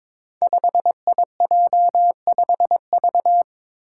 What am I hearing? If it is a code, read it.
Morse code, heard as a keyed tone, about 22 wpm, 700 Hz. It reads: 5IJ5V
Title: Five Challenging Letter-Number Combinations 22wpm